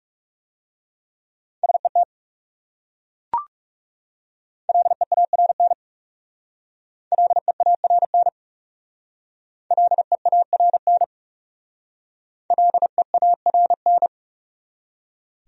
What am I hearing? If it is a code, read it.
Morse code, heard as a keyed tone, about 39 wpm, 700 Hz. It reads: SET LEARN LEARN LEARN LEARN